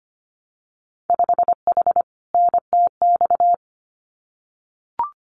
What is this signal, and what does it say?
Morse code, heard as a keyed tone, about 25 wpm, 700 Hz. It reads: 5H NTX
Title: Winter Field Day Exchanges Encore 25wpm